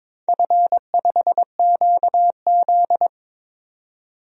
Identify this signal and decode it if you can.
Morse code, heard as a keyed tone, about 22 wpm, 700 Hz. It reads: F5QZ
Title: DXCC-100 Call Signs - 4 Characters - 22wpm